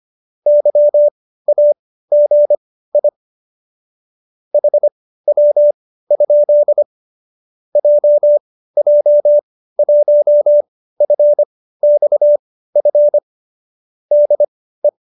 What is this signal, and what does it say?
Morse code, heard as a keyed tone, about 25 wpm, spaced as for 15 wpm, 590 Hz. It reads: YAGI HW? JJ1FXF DE